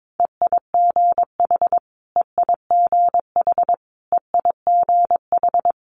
Morse code, heard as a keyed tone, about 22 wpm, 700 Hz. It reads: EIGH EIGH EIGH
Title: Multi-letter Phonemes Repeated 3x 22wpm